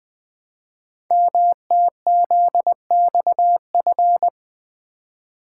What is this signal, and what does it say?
Morse code, heard as a keyed tone, about 20 wpm, 700 Hz. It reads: MTZXF
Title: Code Groups - 5 Letters - 20wpm